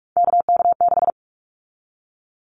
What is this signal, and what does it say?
Morse code, heard as a keyed tone, about 46 wpm, 700 Hz. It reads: KK6
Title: Callsign Prefixes for Top 25 DXCC - 45wpm